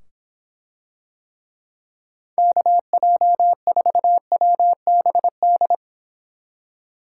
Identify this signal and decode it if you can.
Morse code, heard as a keyed tone, about 25 wpm, 700 Hz. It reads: KJ4WBD